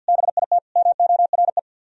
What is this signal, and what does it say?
Morse code, heard as a keyed tone, about 61 wpm, 700 Hz. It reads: BIT MORE
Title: Speed-Racing - Sets of 4 Words 50wpm